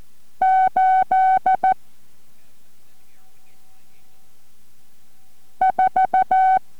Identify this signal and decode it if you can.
Morse code, heard as a keyed tone, about 14 wpm, spaced as for 2 wpm, 760 Hz. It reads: 84